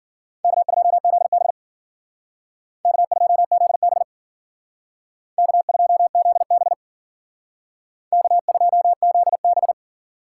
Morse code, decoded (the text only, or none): K2ZB K2ZB K2ZB K2ZB